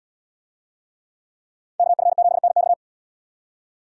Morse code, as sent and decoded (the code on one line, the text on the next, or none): -.- ..-. --... -. -..-
KF7NX